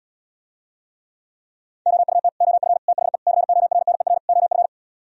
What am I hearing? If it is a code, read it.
Morse code, heard as a keyed tone, about 77 wpm, 700 Hz. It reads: OUT OF THE CORNER OF